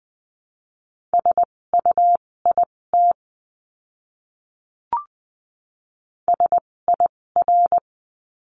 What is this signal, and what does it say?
Morse code, heard as a keyed tone, about 20 wpm, spaced as for 16 wpm, 700 Hz. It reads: SUIT SIR
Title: 256 Lesson 42a - Words 12-20wpm